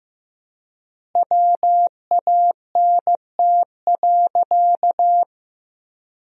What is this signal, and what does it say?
Morse code, heard as a keyed tone, about 15 wpm, 700 Hz. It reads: WANT.
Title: 4x Word-Spacing - Sentences from Top 200 Words - 15wpm